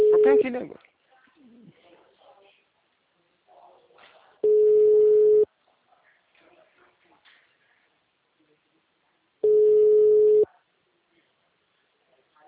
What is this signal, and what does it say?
Ringback tone (Europe, 425 Hz), with speech.